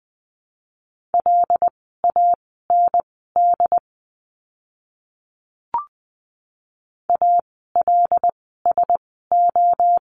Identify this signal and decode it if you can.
Morse code, heard as a keyed tone, about 20 wpm, spaced as for 15 wpm, 700 Hz. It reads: LAND ALSO
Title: 101 Lesson 17f - Letter u - Words 20wpm